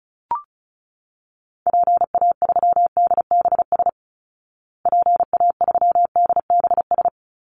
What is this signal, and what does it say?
Morse code, heard as a keyed tone, about 34 wpm, 700 Hz. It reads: PA3DBS PA3DBS
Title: Call Signs Repeated 3x - 35wpm